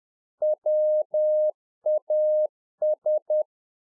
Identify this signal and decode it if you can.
Morse code, heard as a keyed tone, about 10 wpm, 610 Hz. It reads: WAS